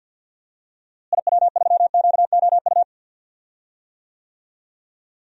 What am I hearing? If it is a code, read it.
Morse code, heard as a keyed tone, about 52 wpm, 700 Hz. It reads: IW3QOU
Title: DXCC-100 Call Signs - 6 Characters - Repeated 3x - 50wpm